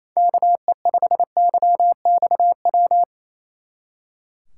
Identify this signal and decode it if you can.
Morse code, heard as a keyed tone, about 35 wpm, 700 Hz. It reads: KE5YXW
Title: Speed Racing - USA 2x3 Call Signs 28wpm